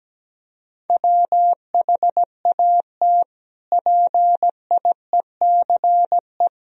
Morse code, heard as a keyed tone, about 17 wpm, 700 Hz. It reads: WHAT PIECE
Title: Sentences from Top 1000 Words - Review Part 2 - 17wpm